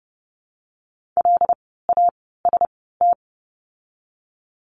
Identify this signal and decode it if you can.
Morse code, heard as a keyed tone, about 30 wpm, spaced as for 17 wpm, 700 Hz. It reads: LAST
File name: Lesson 17f - Letter u - Words 30wpm